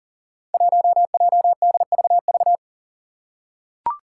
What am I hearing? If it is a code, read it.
Morse code, heard as a keyed tone, about 41 wpm, 700 Hz. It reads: 1JDVV
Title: Five Challenging Letter-Number Combinations 40wpm